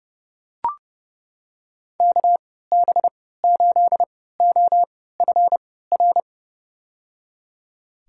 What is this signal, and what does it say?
Morse code, heard as a keyed tone, about 30 wpm, spaced as for 17 wpm, 700 Hz. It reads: KB8OFR
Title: USA Call Signs - 6 Characters - 3x Character Spacing - 30wpm